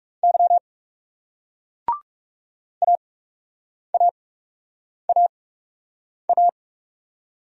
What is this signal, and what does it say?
Morse code, heard as a keyed tone, about 39 wpm, spaced as for 8 wpm, 700 Hz. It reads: Y AAAA